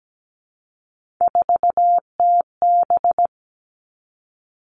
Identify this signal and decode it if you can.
Morse code, heard as a keyed tone, about 17 wpm, 700 Hz. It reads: 4TB
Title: Three Challenging Letter-Number Combinations 17wpm